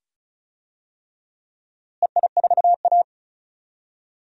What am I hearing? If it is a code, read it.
Morse code, heard as a keyed tone, about 36 wpm, 700 Hz. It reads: EI4A
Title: DXCC-100 Call Signs - 4 Characters - Repeated 2x - 35wpm